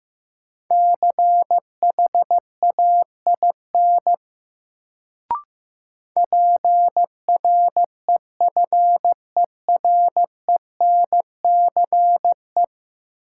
Morse code, heard as a keyed tone, about 15 wpm, 700 Hz. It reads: CHAIN PREFERENCE
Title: Sentences from Top 1500 Words - New Word Drill - 15wpm